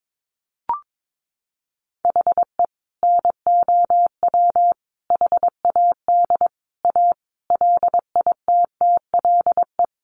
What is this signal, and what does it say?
Morse code, heard as a keyed tone, about 22 wpm, 700 Hz. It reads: HE NOW HAD A LITTLE
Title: Sentences from Top 1500 Words 22wpm